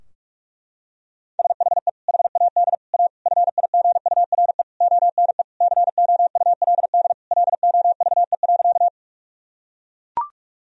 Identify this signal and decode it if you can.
Morse code, heard as a keyed tone, about 43 wpm, 700 Hz. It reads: SHE HAD A FIGURE ONE COULD LOVE.
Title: Sentences from Top 1000 Words - Review Part 2 - 45wpm